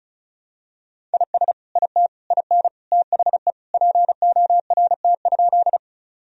Morse code, heard as a keyed tone, about 36 wpm, 700 Hz. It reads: IS IT IN THE PORT?
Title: Sentences from Top 1000 Words - Review Part 1 - 35wpm